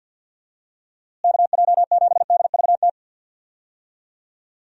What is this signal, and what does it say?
Morse code, heard as a keyed tone, about 52 wpm, 700 Hz. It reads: KJ7DVT
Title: USA Call Signs - 6 Characters - Repeated 3x - 50wpm